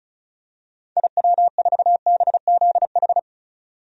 Morse code, heard as a keyed tone, about 36 wpm, 700 Hz. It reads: IW4BZH
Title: DXCC-100 Call Signs - 6 Characters - 35wpm